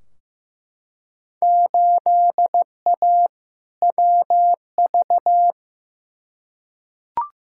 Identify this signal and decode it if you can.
Morse code, heard as a keyed tone, about 15 wpm, 700 Hz. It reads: 8A WV
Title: Field Day Exchanges Encore - 15wpm